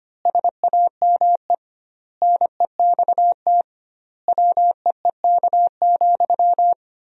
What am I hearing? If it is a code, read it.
Morse code, heard as a keyed tone, about 25 wpm, 700 Hz. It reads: SAME NEXT WEEK,
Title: 2x Word-Spacing - Sentences from Top 500 Words - 25wpm